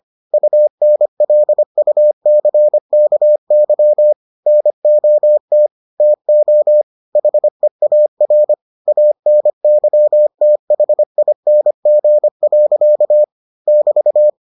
Morse code, heard as a keyed tone, about 25 wpm, 600 Hz. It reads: UNLUCKY NOT TO HEAR ANYTHING. <BT>